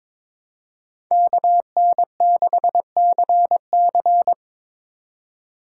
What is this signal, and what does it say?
Morse code, heard as a keyed tone, about 21 wpm, 700 Hz. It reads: KN6CC
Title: USA Call Signs - 5 Characters - 4x Character Spacing - 22wpm